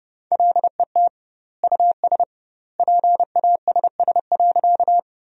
Morse code, heard as a keyed tone, about 30 wpm, 700 Hz. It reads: LET US PASS.
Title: 2x Word-Spacing - Sentences from Top 400 Words - 30wpm